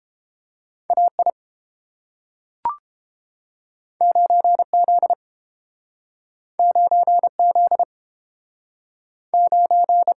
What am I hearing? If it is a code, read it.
Morse code, heard as a keyed tone, about 31 wpm, 700 Hz. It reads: AI 9Z 9Z 9